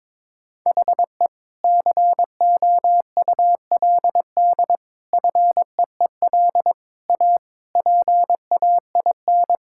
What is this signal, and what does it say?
Morse code, heard as a keyed tone, about 22 wpm, 700 Hz. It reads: HE COULD FEEL A PAIN
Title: Sentences from Top 1600 Words 22wpm